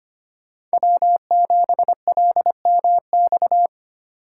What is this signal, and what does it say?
Morse code, heard as a keyed tone, about 30 wpm, 700 Hz. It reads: W7LMX
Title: Speed Racing - USA 1x3 Call Signs 25wpm